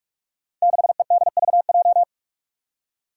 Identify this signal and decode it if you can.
Morse code, heard as a keyed tone, about 44 wpm, 700 Hz. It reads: 6EDVJ
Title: Code Groups - 5 Letter:Numbers - 45wpm